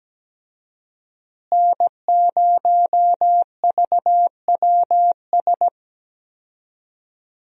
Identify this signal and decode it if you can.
Morse code, heard as a keyed tone, about 17 wpm, 700 Hz. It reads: N0VWS
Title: USA Call Signs - 5 Characters - Repeated 3x - 17wpm